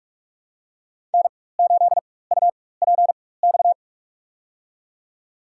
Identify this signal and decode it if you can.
Morse code, heard as a keyed tone, about 47 wpm, spaced as for 21 wpm, 700 Hz. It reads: N8UPX
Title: USA Call Signs - 5 Characters - 4x Character Spacing - 45wpm